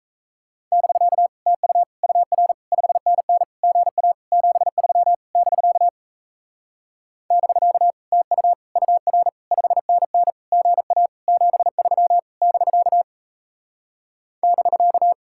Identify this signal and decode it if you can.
Morse code, heard as a keyed tone, about 37 wpm, 700 Hz. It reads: <BK> TU UR 5NN GA 73 <BK> <BK> TU UR 5NN GA 73 <BK> <BK>